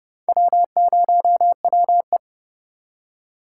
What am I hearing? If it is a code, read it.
Morse code, heard as a keyed tone, about 29 wpm, 700 Hz. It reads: W0WE